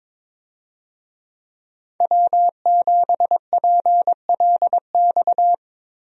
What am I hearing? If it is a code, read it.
Morse code, heard as a keyed tone, about 21 wpm, 700 Hz. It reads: W7PLX
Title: USA Call Signs - 5 Characters - Repeated 3x - 22wpm